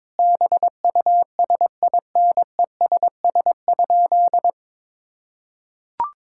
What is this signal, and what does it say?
Morse code, heard as a keyed tone, about 22 wpm, 700 Hz. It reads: BUSINESS?
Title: Sentences from Top 1000 Words - Review Part 6 - 22wpm